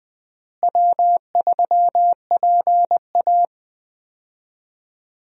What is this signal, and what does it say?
Morse code, heard as a keyed tone, about 21 wpm, 700 Hz. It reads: W3PA
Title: USA Call Signs - 4 Characters - Repeated 3x - 20wpm